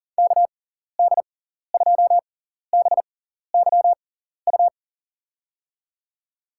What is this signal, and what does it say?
Morse code, heard as a keyed tone, about 40 wpm, spaced as for 14 wpm, 700 Hz. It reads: KD2BYU
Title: USA Call Signs - 6 Characters - 6x Character Spacing - 40wpm